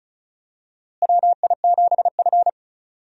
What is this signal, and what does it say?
Morse code, heard as a keyed tone, about 36 wpm, 700 Hz. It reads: WI7F
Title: USA Call Signs - 4 Characters - Repeated 3x - 35wpm